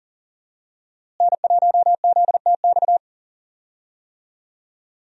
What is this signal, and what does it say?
Morse code, heard as a keyed tone, about 40 wpm, 700 Hz. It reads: N1ZTX
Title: USA Call Signs - 5 Characters - Repeated 3x - 40wpm